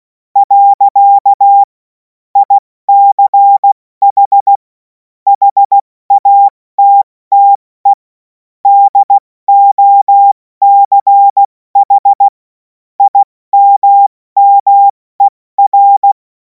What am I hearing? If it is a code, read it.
Morse code, heard as a keyed tone, about 16 wpm, 800 Hz. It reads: . ICH HATTE DOCH IMMER
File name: Morsekurs / Witze / Witz 10. Februar 2026 70 F80